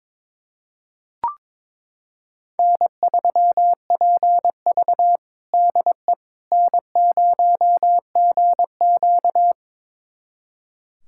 Morse code, heard as a keyed tone, about 22 wpm, 700 Hz. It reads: N3PV DE N0GQ
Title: USA Callsign DE Callsign - 4 Characters - 22wpm